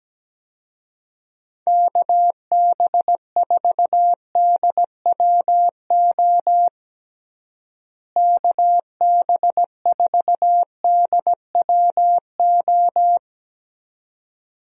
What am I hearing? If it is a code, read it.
Morse code, heard as a keyed tone, about 17 wpm, 700 Hz. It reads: KB4DWO KB4DWO
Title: USA Call Signs - 6 Characters - Repeated 2x - 17wpm